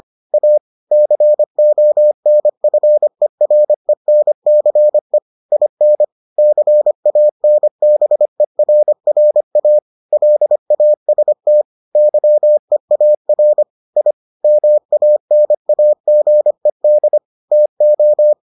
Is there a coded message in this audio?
Yes: Morse code from 0.3 to 18.3 s, heard as a keyed tone, about 25 wpm, 600 Hz. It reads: A CONFERENCE IN CANBERRA LAST YEAR I MANAGED TO